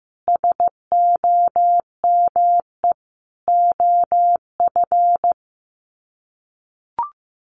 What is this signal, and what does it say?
Morse code, heard as a keyed tone, about 15 wpm, 700 Hz. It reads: SOME OF
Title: Sets of 5 Words Encore2 15wpm